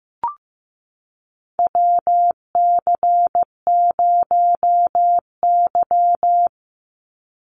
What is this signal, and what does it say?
Morse code, heard as a keyed tone, about 15 wpm, 700 Hz. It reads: WC0Y
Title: NA SOTA Chaser Callsigns 15wpm